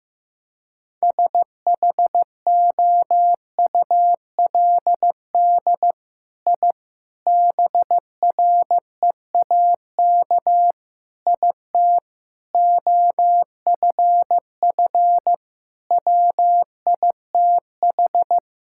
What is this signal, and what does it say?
Morse code, heard as a keyed tone, about 15 wpm, 700 Hz. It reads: SHOULD I BREAK IT OFF WITH
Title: Sentences from Top 1000 Words - Review Part 6 - 15wpm